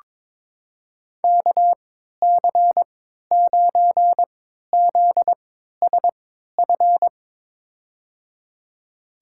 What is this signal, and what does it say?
Morse code, heard as a keyed tone, about 22 wpm, spaced as for 13 wpm, 700 Hz. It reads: KC9ZSF
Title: USA Call Signs - 6 Characters - 3x Character Spacing - 22wpm